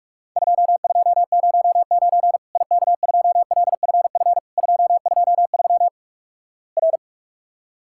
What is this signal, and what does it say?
Morse code, heard as a keyed tone, about 46 wpm, 700 Hz. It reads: 2209 IK2LFF 223 R
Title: Worked All Europe DX Contest - QTC Exchange - 45wpm